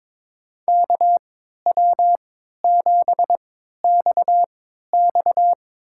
Morse code, heard as a keyed tone, about 21 wpm, spaced as for 12 wpm, 700 Hz. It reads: KW7XX
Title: USA Call Signs - 5 Characters - 3x Character Spacing - 22wpm